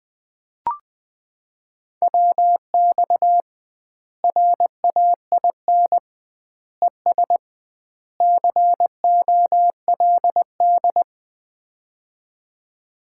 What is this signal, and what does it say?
Morse code, heard as a keyed tone, about 20 wpm, 700 Hz. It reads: WX RAIN ES COLD
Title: Common QSO Phrases - 2x Word-Spacing - 20wpm